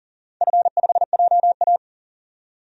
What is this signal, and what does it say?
Morse code, heard as a keyed tone, about 40 wpm, 700 Hz. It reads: F5JA
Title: DXCC-100 Call Signs - 4 Characters - 40wpm